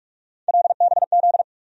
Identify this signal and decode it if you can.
Morse code, heard as a keyed tone, about 46 wpm, 700 Hz. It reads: LBZ
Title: USA Call Sign Postfixes - 3 Letters 45wpm